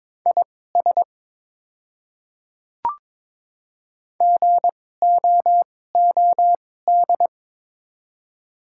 Morse code, heard as a keyed tone, about 22 wpm, spaced as for 16 wpm, 700 Hz. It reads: IS GOOD
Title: 2x Character Spacing - Top 200 Words - 22wpm